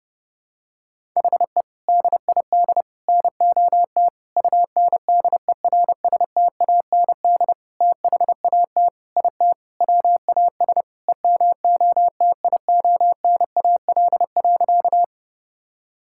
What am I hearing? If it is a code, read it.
Morse code, heard as a keyed tone, about 32 wpm, 700 Hz. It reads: HE DID NOT UNDERSTAND THAT IT WAS EMOTIONAL.
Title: Sentences from Top 1000 Words - Review Part 5 - 30wpm